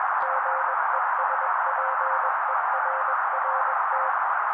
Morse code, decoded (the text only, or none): GESPERRT